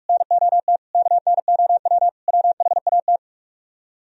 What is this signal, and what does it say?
Morse code, heard as a keyed tone, about 44 wpm, 700 Hz. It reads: NOT KNOW WHAT